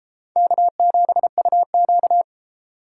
Morse code, heard as a keyed tone, about 29 wpm, 700 Hz. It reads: K7UQ